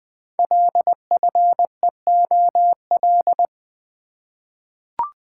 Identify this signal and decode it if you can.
Morse code, heard as a keyed tone, about 20 wpm, 700 Hz. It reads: LFEOL
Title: Code Groups - 5 Letters - 20wpm